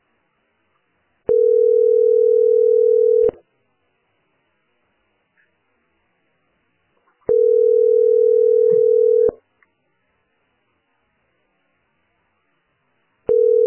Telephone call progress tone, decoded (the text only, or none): ringback tone